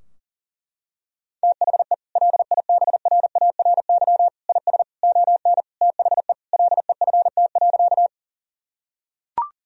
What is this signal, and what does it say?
Morse code, heard as a keyed tone, about 40 wpm, 700 Hz. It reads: THE LIBRARY IS ON THE LEFT.
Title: Sentences from Top 1000 Words - Review Part 4 - 40wpm